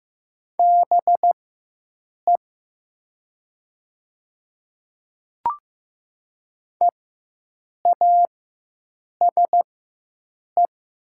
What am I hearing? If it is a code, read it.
Morse code, heard as a keyed tone, about 15 wpm, spaced as for 7 wpm, 700 Hz. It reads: BE EASE
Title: 4x Character Spacing - Top 300 Words - 15wpm